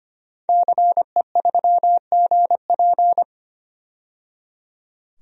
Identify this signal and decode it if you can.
Morse code, heard as a keyed tone, about 25 wpm, 700 Hz. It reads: CE3GP
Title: DXCC-100 Call Signs - 5 Characters - 25wpm